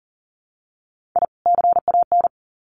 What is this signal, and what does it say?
Morse code, heard as a keyed tone, about 40 wpm, 700 Hz. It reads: I CAN
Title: Sets of 2 Words 40wpm